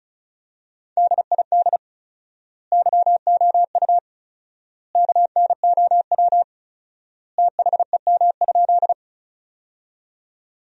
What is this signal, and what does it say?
Morse code, heard as a keyed tone, about 34 wpm, 700 Hz. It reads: DID YOU KNOW THEM?
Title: 4x Word-Spacing - Sentences from Top 100 Words - 35wpm